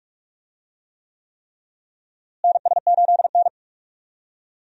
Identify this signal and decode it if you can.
Morse code, heard as a keyed tone, about 46 wpm, 700 Hz. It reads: NS8N
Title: USA Call Signs - 4 Characters - Repeated 3x - 45wpm